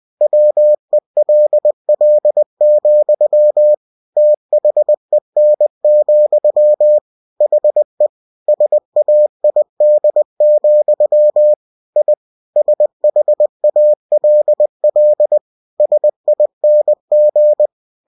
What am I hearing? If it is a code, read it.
Morse code, heard as a keyed tone, about 20 wpm, 600 Hz. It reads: WELL, THEN, HE SAID, I SHALL SING
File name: cw-books / winnie-the-pooh / winnie-20wpm-page-0075